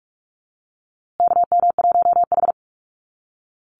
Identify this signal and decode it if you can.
Morse code, heard as a keyed tone, about 44 wpm, 700 Hz. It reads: KM1H
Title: USA Call Signs Repeated 3x 45wpm